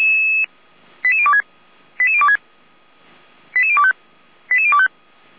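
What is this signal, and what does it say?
Five-tone selective call (ZVEI), heard as quick runs of five tones, tones of about 70 ms. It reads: 88026 88026 88025 88025
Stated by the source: Five-tone ZVEI selective call received by SDR.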